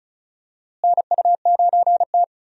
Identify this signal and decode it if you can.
Morse code, heard as a keyed tone, about 34 wpm, 700 Hz. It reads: NU9T
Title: USA Call Signs - 4 Characters - Repeated 2x - 35wpm